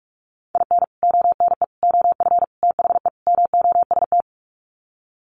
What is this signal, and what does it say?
Morse code, heard as a keyed tone, about 44 wpm, 700 Hz. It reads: IN ONE OF THE MOST
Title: Sets of 5 Words Encore 45wpm